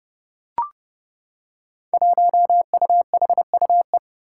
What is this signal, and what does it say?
Morse code, heard as a keyed tone, about 31 wpm, 700 Hz. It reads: 1UHUE
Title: Five Challenging Letter-Number Combinations 30wpm